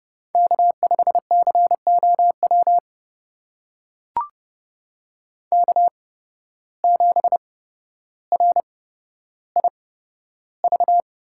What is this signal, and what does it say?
Morse code, heard as a keyed tone, about 30 wpm, spaced as for 13 wpm, 700 Hz. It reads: K5COW K7RIV
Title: USA Call Signs - 5 Characters - 8x Character Spacing - 30wpm